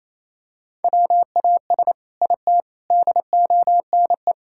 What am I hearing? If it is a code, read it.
Morse code, heard as a keyed tone, about 28 wpm, 700 Hz. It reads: WAS IT DONE
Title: Sentences from Top 1000 Words - Review Part 6 - 28wpm